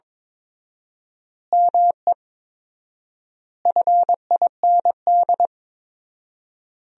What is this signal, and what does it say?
Morse code, heard as a keyed tone, about 22 wpm, 700 Hz. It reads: ME FIND
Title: 4x Word-Spacing - Sentences from Top 200 Words - 22wpm